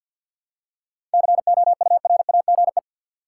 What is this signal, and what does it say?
Morse code, heard as a keyed tone, about 52 wpm, 700 Hz. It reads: COURAGE
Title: Seven Letter Words - 50wpm